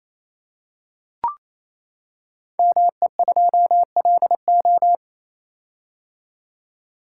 Morse code, heard as a keyed tone, about 28 wpm, 700 Hz. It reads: ME2LO
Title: Code Groups - 5 Letter:Numbers - 28wpm